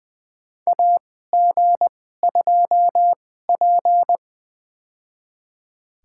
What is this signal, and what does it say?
Morse code, heard as a keyed tone, about 20 wpm, spaced as for 14 wpm, 700 Hz. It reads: AG2P
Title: USA Call Signs - 2x Character Spacing - 20wpm